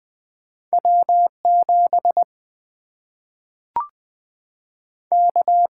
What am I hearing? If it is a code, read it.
Morse code, heard as a keyed tone, about 20 wpm, 700 Hz. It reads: W7 K
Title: Single-Letter Single-Number 20wpm